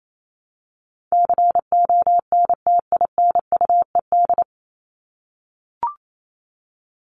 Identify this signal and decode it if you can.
Morse code, heard as a keyed tone, about 28 wpm, 700 Hz. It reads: CONTINUED
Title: ED Suffix - Top 100 Words with Base Words 28wpm